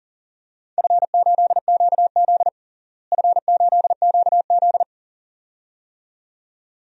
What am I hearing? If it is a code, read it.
Morse code, heard as a keyed tone, about 40 wpm, 700 Hz. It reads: F8QZ F8QZ
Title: DXCC-100 Call Signs - 4 Characters - Repeated 2x - 40wpm